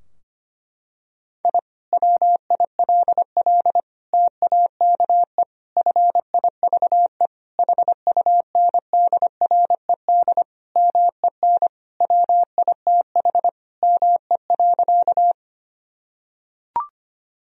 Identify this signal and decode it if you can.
Morse code, heard as a keyed tone, about 25 wpm, 700 Hz. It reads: I WILL TAKE FIVE HUNDRED MEN WITH ME.
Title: Sentences from Top 1000 Words - Review Part 2 - 25wpm